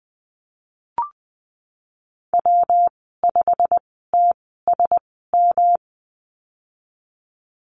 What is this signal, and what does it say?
Morse code, heard as a keyed tone, about 20 wpm, spaced as for 15 wpm, 700 Hz. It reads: W5TSM
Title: 147 Lesson 24g - Letter y - Call Signs 20wpm